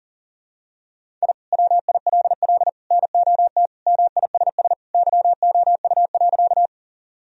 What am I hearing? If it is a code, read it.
Morse code, heard as a keyed tone, about 40 wpm, 700 Hz. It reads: I WILL NOT MISS YOU.
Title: Sentences from Top 1000 Words - Review Part 3 - 40wpm